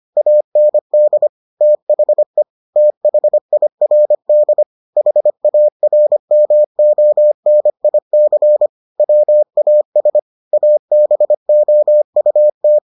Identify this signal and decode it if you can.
Morse code, heard as a keyed tone, about 25 wpm, 600 Hz. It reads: AND THE THIRD HARMONIC WAS ABOUT